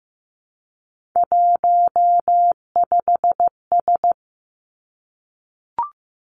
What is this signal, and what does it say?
Morse code, heard as a keyed tone, about 15 wpm, 700 Hz. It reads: 15S
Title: Three Challenging Letter-Number Combinations 15wpm